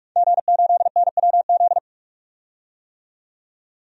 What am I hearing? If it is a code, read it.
Morse code, heard as a keyed tone, about 43 wpm, 700 Hz. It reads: G8NWZ
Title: DXCC-100 Call Signs - 5 Characters - 45wpm